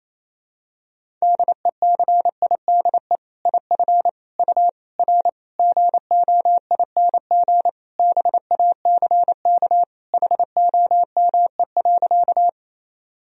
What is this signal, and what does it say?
Morse code, heard as a keyed tone, about 28 wpm, 700 Hz. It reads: DECIDE IF U R GOING BACK HOME.
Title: Sentences from Top 500 Words and Abbreviations 28wpm